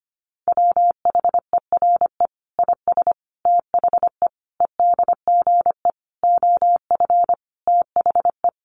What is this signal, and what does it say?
Morse code, heard as a keyed tone, about 25 wpm, 700 Hz. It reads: WHERE IS THE EDGE OF THE
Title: Sentences from Top 1300 Words 25wpm